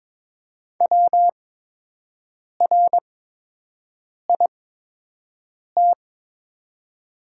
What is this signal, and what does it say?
Morse code, heard as a keyed tone, about 22 wpm, spaced as for 6 wpm, 700 Hz. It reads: WRIT
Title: 8x Character Spacing - Top 500 Words - 22wpm